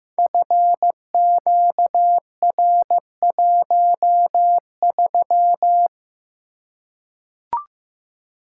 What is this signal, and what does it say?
Morse code, heard as a keyed tone, about 15 wpm, 700 Hz. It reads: FQR13
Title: Code Groups - 5 Letter:Numbers - 15wpm